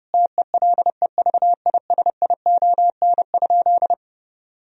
Morse code, heard as a keyed tone, about 30 wpm, 700 Hz. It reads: TELEVISION?
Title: Sentences from Top 1000 Words - Review Part 4 - 30wpm